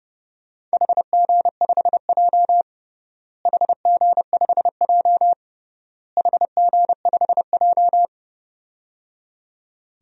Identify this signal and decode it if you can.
Morse code, heard as a keyed tone, about 29 wpm, 700 Hz. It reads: HG5J HG5J HG5J
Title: DXCC-100 Call Signs - 4 Characters - Repeated 3x - 30wpm